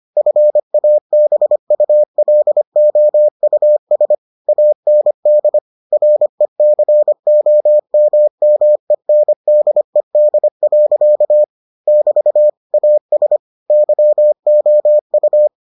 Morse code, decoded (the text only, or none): FABULOUS AND RECOMMENDED. <BT> AS YOU